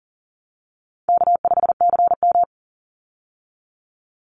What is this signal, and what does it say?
Morse code, heard as a keyed tone, about 40 wpm, 700 Hz. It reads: K5CM
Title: Easy Call Signs Repeated 3x - 40wpm